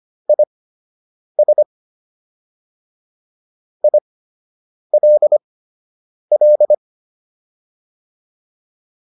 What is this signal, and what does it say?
Morse code, heard as a keyed tone, about 25 wpm, spaced as for 8 wpm, 600 Hz. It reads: IS ILL